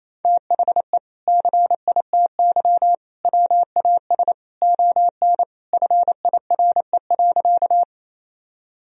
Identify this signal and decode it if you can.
Morse code, heard as a keyed tone, about 27 wpm, 700 Hz. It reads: THE CITY WAS ON FIRE.
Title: Sentences from Top 1000 Words - Review Part 1 - 28wpm